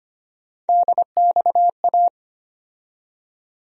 Morse code, heard as a keyed tone, about 25 wpm, 700 Hz. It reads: DXA
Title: USA Call Sign Postfixes - 3 Letters 25wpm